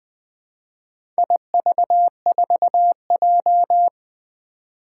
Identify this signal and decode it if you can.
Morse code, heard as a keyed tone, about 20 wpm, 700 Hz. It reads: IV4J
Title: Four Challenging Letter-Number Combinations 20wpm